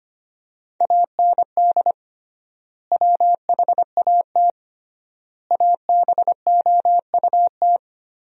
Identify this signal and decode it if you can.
Morse code, heard as a keyed tone, about 25 wpm, 700 Hz. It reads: AND WHAT ABOUT
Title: 3x Word-Spacing - Sentences from Top 200 Words - 25wpm